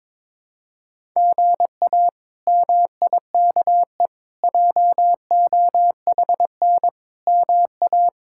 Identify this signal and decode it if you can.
Morse code, heard as a keyed tone, about 22 wpm, 700 Hz. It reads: GA MIKE JOHN MA
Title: K1USN SST QSO Practice - 22wpm